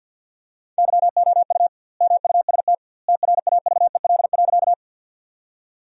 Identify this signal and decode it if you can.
Morse code, heard as a keyed tone, about 50 wpm, 700 Hz. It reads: YOU MUST TRAVEL.
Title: 2x Word-Spacing - Sentences from Top 400 Words - 50wpm